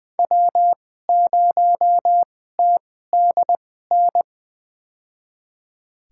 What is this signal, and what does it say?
Morse code, heard as a keyed tone, about 20 wpm, spaced as for 14 wpm, 700 Hz. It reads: W0TDN
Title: USA Call Signs - 5 Characters - 2x Character Spacing - 20wpm